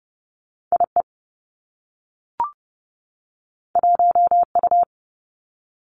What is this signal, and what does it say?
Morse code, heard as a keyed tone, about 30 wpm, 700 Hz. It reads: IE 1U
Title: Two Challenging Letter-Number Combinations 30wpm